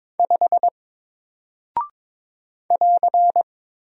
Morse code, heard as a keyed tone, about 22 wpm, 700 Hz. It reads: 5 <AR>
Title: Single Character (Letter:Number:Punctuation:Prosign) Alt 22wpm